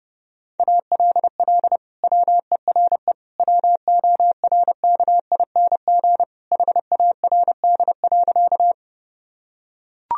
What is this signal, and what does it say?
Morse code, heard as a keyed tone, about 30 wpm, 700 Hz. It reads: ALL WERE WORKING HARD.
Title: Sentences from Top 1000 Words - Review Part 4 - 30wpm